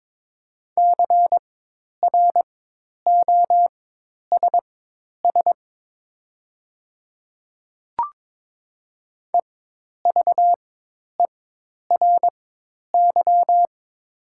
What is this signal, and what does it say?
Morse code, heard as a keyed tone, about 22 wpm, spaced as for 10 wpm, 700 Hz. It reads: CROSS EVERY
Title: 4x Character Spacing - Top 300 Words - 22wpm